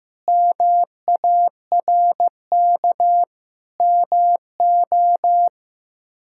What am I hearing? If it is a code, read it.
Morse code, heard as a keyed tone, about 15 wpm, 700 Hz. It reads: MARK MO
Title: K1USN SST QSO Practice - 15wpm